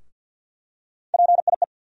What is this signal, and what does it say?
Morse code, heard as a keyed tone, about 48 wpm, 700 Hz. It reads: PIE
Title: USA Call Sign Postfixes - 3 Letters 50wpm